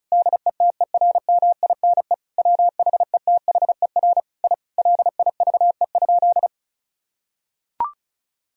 Morse code, heard as a keyed tone, about 34 wpm, 700 Hz. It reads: DETERMINE WHETHER I LIVE?
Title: Sentences from Top 1000 Words - Review Part 4 - 35wpm